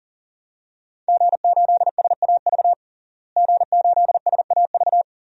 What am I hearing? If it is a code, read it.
Morse code, heard as a keyed tone, about 40 wpm, 700 Hz. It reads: G8SAV G8SAV
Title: DXCC-100 Call Signs - 5 Characters - Repeated 2x - 40wpm